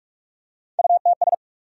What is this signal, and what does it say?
Morse code, heard as a keyed tone, about 43 wpm, 700 Hz. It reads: UTS
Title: USA Call Sign Postfixes - 3 Letters 45wpm